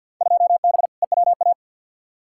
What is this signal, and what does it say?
Morse code, heard as a keyed tone, about 51 wpm, 700 Hz. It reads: 2B EWA
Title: Field Day Exchanges - 50wpm